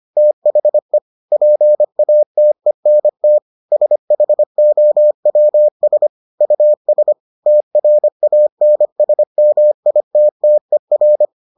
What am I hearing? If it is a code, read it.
Morse code, heard as a keyed tone, about 25 wpm, 600 Hz. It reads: THE PATENT SHOWS US TRANSMITTER